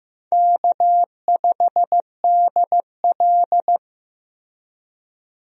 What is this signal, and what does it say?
Morse code, heard as a keyed tone, about 15 wpm, 700 Hz. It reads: K5DL
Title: USA Call Signs - 4 Characters - Repeated 3x - 15wpm